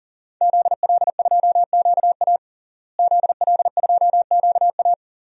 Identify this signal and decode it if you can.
Morse code, heard as a keyed tone, about 40 wpm, 700 Hz. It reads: ZL2QA ZL2QA
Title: DXCC-100 Call Signs - 5 Characters - Repeated 2x - 40wpm